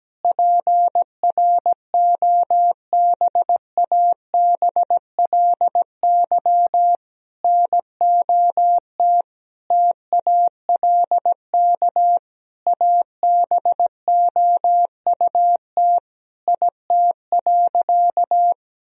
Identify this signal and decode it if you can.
Morse code, heard as a keyed tone, about 17 wpm, 700 Hz. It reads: PROBABLY NOT TALK ABOUT IT.
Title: Sentences from Top 1000 Words - Review Part 4 - 17wpm